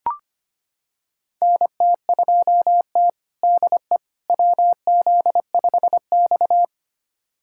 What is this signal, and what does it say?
Morse code, heard as a keyed tone, about 25 wpm, 700 Hz. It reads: NT2T DE WZ5X
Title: USA Callsign DE Callsign - 4 Characters - 25wpm